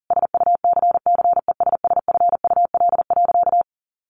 Morse code, heard as a keyed tone, about 40 wpm, 700 Hz. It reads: SUCCESSFUL.
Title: Sentences from Top 1200 Words 40wpm